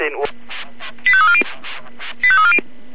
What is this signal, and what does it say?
Five-tone selective call (ZVEI), heard as quick runs of five tones, tones of about 70 ms, with speech. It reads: 85340